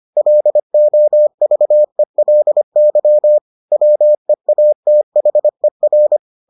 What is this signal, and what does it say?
Morse code, heard as a keyed tone, about 25 wpm, 600 Hz. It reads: LOVELY WEATHER